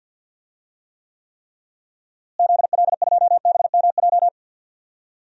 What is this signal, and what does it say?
Morse code, heard as a keyed tone, about 49 wpm, 700 Hz. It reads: ZL2BMJ
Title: DXCC-100 Call Signs - 6 Characters - Repeated 3x - 50wpm